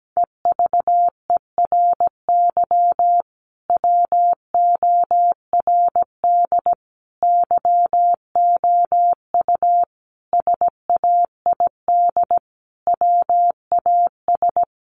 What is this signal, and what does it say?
Morse code, heard as a keyed tone, about 17 wpm, 700 Hz. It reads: EVERY WORD YOU SAID WAS
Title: Sentences from Top 400 Words 17wpm